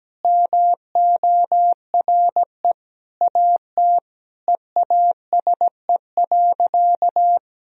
Morse code, heard as a keyed tone, about 17 wpm, 700 Hz. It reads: MORE AT EASE.
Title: Sentences from Top 1000 Words - Review Part 1 - 17wpm